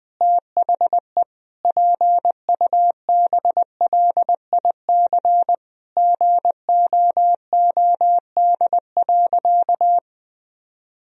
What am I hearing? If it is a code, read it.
Morse code, heard as a keyed tone, about 20 wpm, 700 Hz. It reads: THE PUBLIC GOOD.
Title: Sentences from Top 1000 Words - Review Part 3 - 20wpm